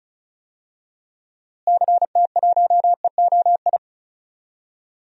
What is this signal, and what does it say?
Morse code, heard as a keyed tone, about 36 wpm, 700 Hz. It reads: CT1EOI
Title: DXCC-100 Call Signs - 6 Characters - Repeated 3x - 35wpm